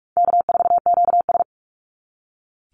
Morse code, heard as a keyed tone, about 44 wpm, 700 Hz. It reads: K4QS
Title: NA SOTA Chaser Callsigns 45wpm